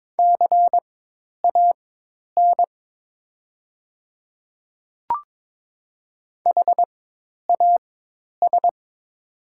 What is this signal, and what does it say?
Morse code, heard as a keyed tone, about 22 wpm, spaced as for 10 wpm, 700 Hz. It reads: CAN HAS
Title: 4x Character Spacing - Top 100 Words - 22wpm